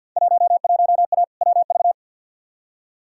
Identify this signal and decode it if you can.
Morse code, heard as a keyed tone, about 51 wpm, 700 Hz. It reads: 11A WV
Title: Field Day Exchanges Encore - 50wpm